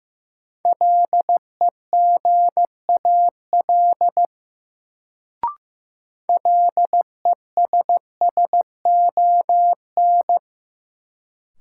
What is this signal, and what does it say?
Morse code, heard as a keyed tone, about 16 wpm, 700 Hz. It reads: LEGAL LESSON
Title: Sentences from Top 1200 Words - New Word Drill Alt - 15wpm